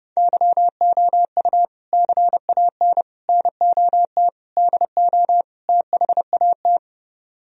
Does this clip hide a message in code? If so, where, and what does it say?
Yes: Morse code from 0.2 to 6.8 s, heard as a keyed tone, about 31 wpm, 700 Hz. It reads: YOU CAN NOT DO THAT